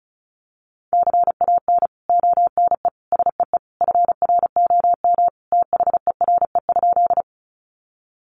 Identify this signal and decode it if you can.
Morse code, heard as a keyed tone, about 34 wpm, 700 Hz. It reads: CAN ONE SEE FROM THERE?
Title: Sentences from Top 100 Words 35wpm